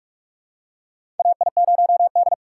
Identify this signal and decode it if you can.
Morse code, heard as a keyed tone, about 47 wpm, 700 Hz. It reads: AI0D
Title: USA Call Signs - 4 Characters - 3x Character Spacing - 45wpm